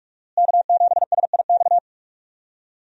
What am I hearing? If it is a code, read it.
Morse code, heard as a keyed tone, about 46 wpm, 700 Hz. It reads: K7SIX
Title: USA Call Signs - 5 Characters - Repeated 2x - 45wpm